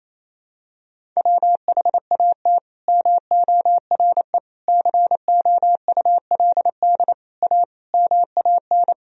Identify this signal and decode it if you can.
Morse code, heard as a keyed tone, about 28 wpm, 700 Hz. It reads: WHAT MORE COULD A MAN
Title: Sentences from Top 200 Words and Abbreviations 28wpm